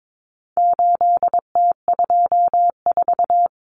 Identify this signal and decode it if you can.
Morse code, heard as a keyed tone, about 22 wpm, 700 Hz. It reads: 8T24
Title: Common Cut Numbers - 4-Digit 22wpm